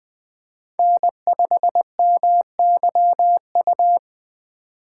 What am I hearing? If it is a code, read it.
Morse code, heard as a keyed tone, about 20 wpm, 700 Hz. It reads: N5MYU
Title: USA Call Signs - 5 Characters - 2x Character Spacing - 20wpm